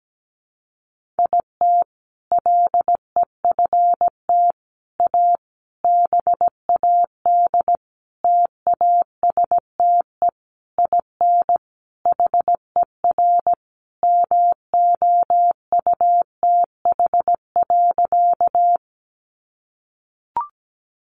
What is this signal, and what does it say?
Morse code, heard as a keyed tone, about 17 wpm, 700 Hz. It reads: IT LEFT A BAD TASTE IN HER MOUTH.
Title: Sentences from Top 1500 Words 17wpm